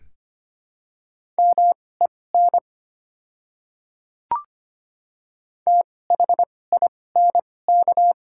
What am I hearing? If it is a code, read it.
Morse code, heard as a keyed tone, about 25 wpm, spaced as for 18 wpm, 690 Hz. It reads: MEN THINK
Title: 2x Character Spacing - Top 200 Words - 25wpm